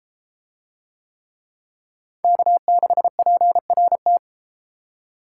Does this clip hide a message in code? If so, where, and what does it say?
Yes: Morse code from 2.2 to 4.2 s, heard as a keyed tone, about 32 wpm, 700 Hz. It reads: K6PRT